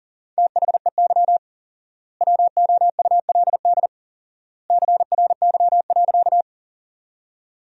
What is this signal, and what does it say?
Morse code, heard as a keyed tone, about 40 wpm, 700 Hz. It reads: THEY WOULD CRY.
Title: 4x Word-Spacing - Sentences from Top 500 Words - 40wpm